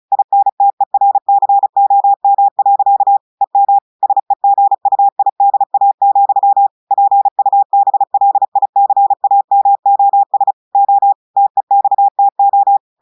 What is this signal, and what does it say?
Morse code, heard as a keyed tone, about 36 wpm, 800 Hz. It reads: INTERCOM. EM SEGUIDA, PUBLICAMOS O TEXTO
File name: audios / portugais / portugais texte en clair 1 vitesse 35 mots minutes